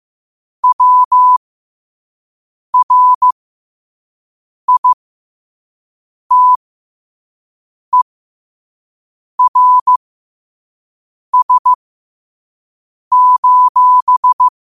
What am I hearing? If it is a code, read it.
Morse code, heard as a keyed tone, about 15 wpm, spaced as for 5 wpm, 1000 Hz. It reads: WRITERS:
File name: qotdfiles / 5wpm